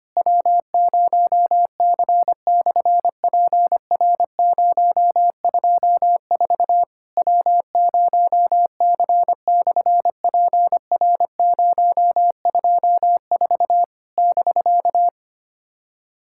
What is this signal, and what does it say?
Morse code, heard as a keyed tone, about 25 wpm, 700 Hz. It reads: W0C/PR024 W0C/PR024 <BK>
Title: SOTA QSO Practice 25wpm